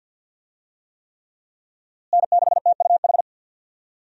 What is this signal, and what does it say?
Morse code, heard as a keyed tone, about 52 wpm, 700 Hz. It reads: N6TUH
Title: USA Call Signs - 5 Characters - Repeated 3x - 50wpm